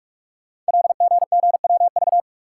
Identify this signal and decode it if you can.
Morse code, heard as a keyed tone, about 44 wpm, 700 Hz. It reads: LGGWV
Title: Code Groups - 5 Letters - 45wpm